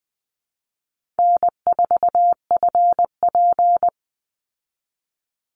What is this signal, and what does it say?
Morse code, heard as a keyed tone, about 20 wpm, 700 Hz. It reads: N4FP
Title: Easy Call Signs Repeated 2x - 20wpm